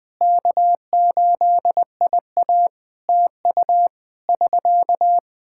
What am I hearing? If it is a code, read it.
Morse code, heard as a keyed tone, about 20 wpm, 700 Hz. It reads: K8IA TU <SK>
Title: Practice QSO - Exchange 4 - Ending - 20wpm